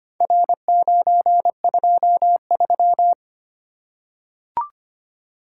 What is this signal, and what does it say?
Morse code, heard as a keyed tone, about 25 wpm, 700 Hz. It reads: R923
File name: practice / 25wpm / Four Letter-Number Combinations - 25wpm